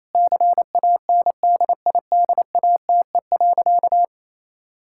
Morse code, heard as a keyed tone, about 28 wpm, 700 Hz. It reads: CANDIDATE.
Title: Sentences from Top 1600 Words 28wpm